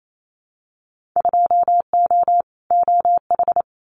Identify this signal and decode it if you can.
Morse code, heard as a keyed tone, about 28 wpm, 700 Hz. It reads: 2O OH
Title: Winter Field Day Exchanges 28wpm